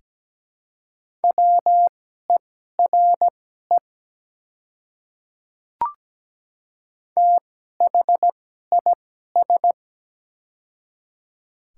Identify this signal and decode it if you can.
Morse code, heard as a keyed tone, about 17 wpm, spaced as for 12 wpm, 700 Hz. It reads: WERE THIS
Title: 2x Character Spacing - Top 100 Words - 17wpm